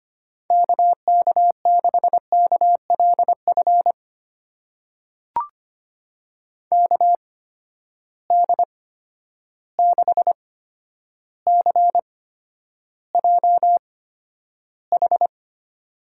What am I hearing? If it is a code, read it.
Morse code, heard as a keyed tone, about 25 wpm, spaced as for 11 wpm, 700 Hz. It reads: KK6KLF KD6CJH
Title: USA Call Signs - 8x Character Spacing - 25wpm